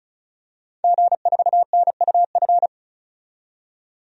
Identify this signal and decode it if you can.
Morse code, heard as a keyed tone, about 36 wpm, 700 Hz. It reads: G4NUF
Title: DXCC-100 Call Signs - 5 Characters - 35wpm